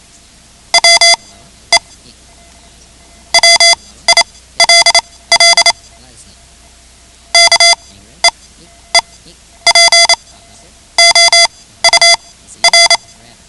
Morse code, heard as a keyed tone, about 28 wpm, spaced as for 14 wpm, 2050 Hz. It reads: WE WILL KEEP OUR
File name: ham radio / CodeQuick / Code Quick 10